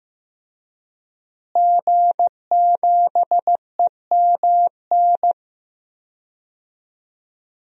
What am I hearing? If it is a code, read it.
Morse code, heard as a keyed tone, about 15 wpm, 700 Hz. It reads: G7EMN
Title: DXCC-100 Call Signs - 5 Characters - Repeated 3x - 15wpm